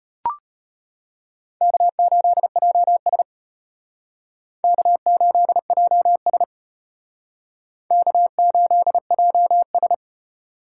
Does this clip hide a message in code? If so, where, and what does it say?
Yes: Morse code from 1.6 to 9.9 s, heard as a keyed tone, about 32 wpm, 700 Hz. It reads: K8JS K8JS K8JS